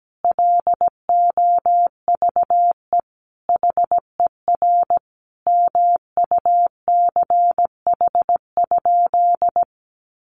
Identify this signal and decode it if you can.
Morse code, heard as a keyed tone, about 17 wpm, 700 Hz. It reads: LOVE HER MUCH?
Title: Sentences from Top 400 Words 17wpm